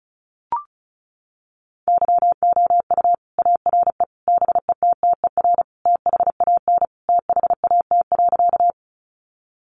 Morse code, heard as a keyed tone, about 34 wpm, 700 Hz. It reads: YOU ARE BETTER THAN THAT.
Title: Sentences from Top 400 Words 35wpm